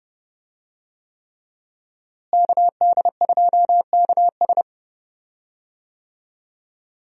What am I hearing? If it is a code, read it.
Morse code, heard as a keyed tone, about 30 wpm, 700 Hz. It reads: KD2KS
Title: USA Call Signs - 5 Characters - Repeated 3x - 30wpm